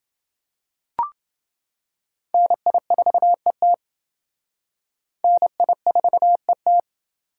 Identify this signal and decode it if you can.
Morse code, heard as a keyed tone, about 28 wpm, 700 Hz. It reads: NI4ET NI4ET